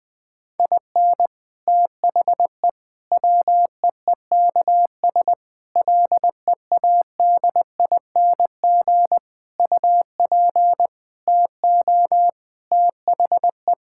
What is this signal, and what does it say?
Morse code, heard as a keyed tone, about 21 wpm, 700 Hz. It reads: IN THE WEEKS LEADING UP TO THE
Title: Sentences with 6-Word Phrases - 20wpm